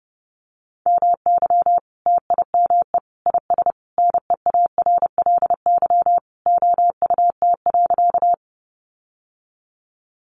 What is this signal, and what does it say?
Morse code, heard as a keyed tone, about 30 wpm, 700 Hz. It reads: MY TIME IS NEARLY OUT.
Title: Sentences from Top 1300 Words 30wpm